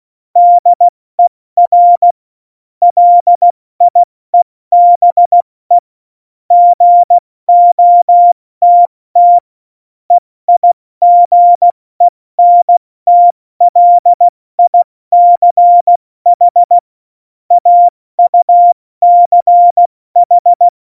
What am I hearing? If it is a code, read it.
Morse code, heard as a keyed tone, about 16 wpm, 700 Hz. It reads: DER LIEBE GOTT EIGENTLICH AUCH
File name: Morsekurs / Witze / Witz 10. Oktober 17, Tempo 70